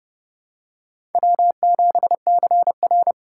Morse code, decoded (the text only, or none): W7CR